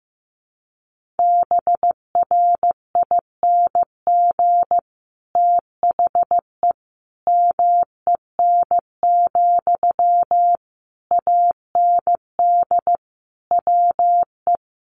Morse code, heard as a keyed tone, about 15 wpm, 700 Hz. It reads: BRING THE MEN, AND WE
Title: Sentences from Top 1400 Words 15wpm